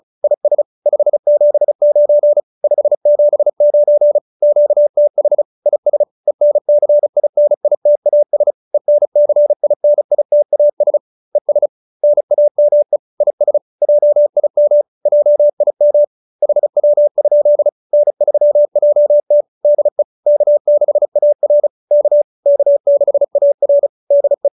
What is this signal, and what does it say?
Morse code, heard as a keyed tone, about 35 wpm, 590 Hz. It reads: IS 579 579 QTH IS ENCINITAS ENCINITAS ES NAME IS JIM JIM HW? N3JT DE K6AR K K6AR DE